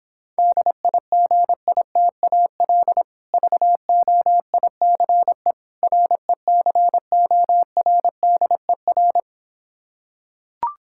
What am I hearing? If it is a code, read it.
Morse code, heard as a keyed tone, about 26 wpm, 700 Hz. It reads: DIGITAL VOICE RECORDER